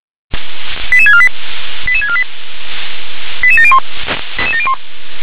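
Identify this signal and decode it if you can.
Five-tone selective call (ZVEI), heard as quick runs of five tones, tones of about 70 ms. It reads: 88648 88648 88701
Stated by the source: ZVEI five-tone selective call received by SDR.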